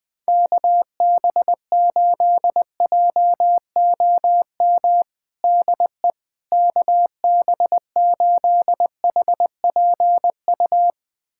Morse code, heard as a keyed tone, about 21 wpm, 700 Hz. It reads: KB8JOM DE KB8HPU
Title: USA Callsign DE Callsign - 6 Characters - 20wpm